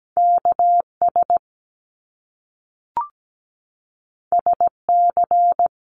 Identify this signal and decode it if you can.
Morse code, heard as a keyed tone, about 17 wpm, 700 Hz. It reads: KS SC
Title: US States 17wpm